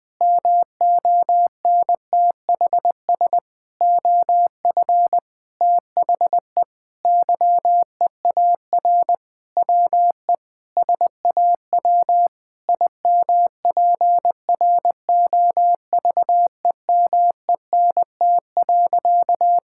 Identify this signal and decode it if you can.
Morse code, heard as a keyed tone, about 20 wpm, 700 Hz. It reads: MONTHS OF THE YEAR WE SAW IMPROVEMENT.
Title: Sentences with 8-Word Phrases - Spoken Beginning - 20wpm